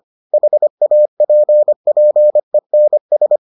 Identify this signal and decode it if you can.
Morse code, heard as a keyed tone, about 25 wpm, 600 Hz. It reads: HAPPENS